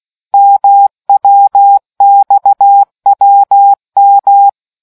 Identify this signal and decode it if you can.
Morse code, heard as a keyed tone, about 16 wpm, 800 Hz. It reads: MWXWM